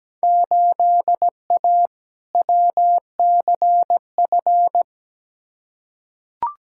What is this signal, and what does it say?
Morse code, heard as a keyed tone, about 17 wpm, 700 Hz. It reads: 8A WCF
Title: Field Day Exchanges Encore - 17wpm